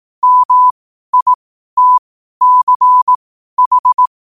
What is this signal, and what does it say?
Morse code, heard as a keyed tone, about 18 wpm, spaced as for 12 wpm, 1000 Hz. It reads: MITCH
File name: qotdfiles / 12wpm